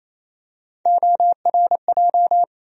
Morse code, heard as a keyed tone, about 28 wpm, 700 Hz. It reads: ORJ
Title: USA Call Sign Postfixes - 3 Letters 28wpm